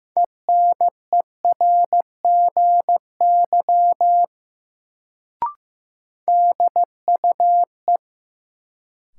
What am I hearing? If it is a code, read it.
Morse code, heard as a keyed tone, about 15 wpm, 700 Hz. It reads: ENERGY DUE
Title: Sentences from Top 900 Words - New Word Drill - 15wpm